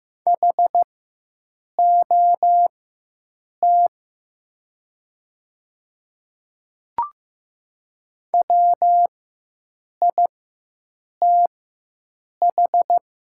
Morse code, heard as a keyed tone, about 15 wpm, spaced as for 7 wpm, 690 Hz. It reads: HOT WITH
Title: 4x Character Spacing - Top 500 Words - 15wpm